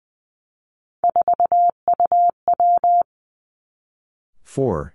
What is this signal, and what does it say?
Morse code, heard as a keyed tone, about 20 wpm, 700 Hz. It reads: 4UW
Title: Three Challenging Letter-Number Combinations 20wpm